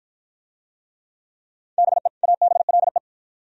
Morse code, heard as a keyed tone, about 45 wpm, 700 Hz. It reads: BE ABLE